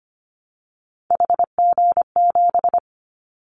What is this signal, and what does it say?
Morse code, heard as a keyed tone, about 25 wpm, 700 Hz. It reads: HG7
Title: Callsign Prefixes for Top 25 DXCC - 25wpm